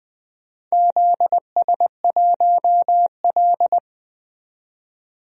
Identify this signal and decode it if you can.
Morse code, heard as a keyed tone, about 20 wpm, 700 Hz. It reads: ZS1L
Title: DXCC-100 Call Signs - 4 Characters - 20wpm